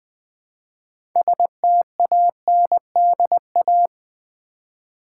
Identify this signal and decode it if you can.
Morse code, heard as a keyed tone, about 20 wpm, 700 Hz. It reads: STANDA
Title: Build A Word - 3 Letters At A Time - 8-Letter Words - 20wpm